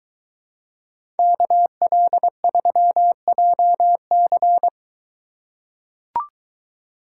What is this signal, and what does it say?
Morse code, heard as a keyed tone, about 21 wpm, 700 Hz. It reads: KL3JC